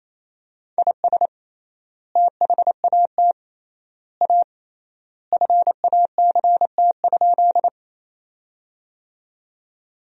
Morse code, heard as a keyed tone, about 28 wpm, 700 Hz. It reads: IS THAT A FACT?
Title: 3x Word-Spacing - Sentences from Top 500 Words - 28wpm